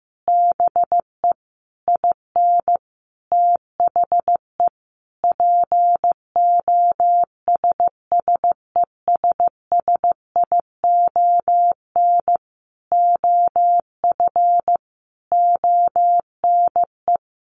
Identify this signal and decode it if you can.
Morse code, heard as a keyed tone, about 15 wpm, 700 Hz. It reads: BE IN THE POSSESSION OF ONE
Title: Sentences from Top 1400 Words 15wpm